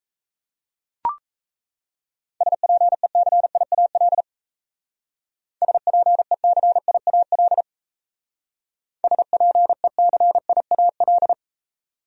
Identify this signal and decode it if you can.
Morse code, heard as a keyed tone, about 37 wpm, 700 Hz. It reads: SPECIAL SPECIAL SPECIAL